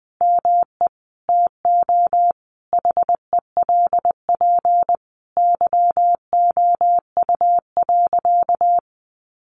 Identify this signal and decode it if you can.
Morse code, heard as a keyed tone, about 20 wpm, 700 Hz. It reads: ME TO HELP YOU.
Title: Sentences from Top 1200 Words 20wpm